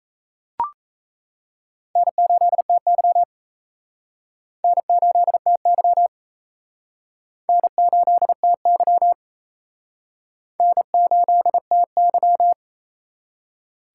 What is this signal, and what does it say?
Morse code, heard as a keyed tone, about 35 wpm, 700 Hz. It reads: N8TY N8TY N8TY N8TY
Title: Speed Racing - USA 1x2 Call Signs 28wpm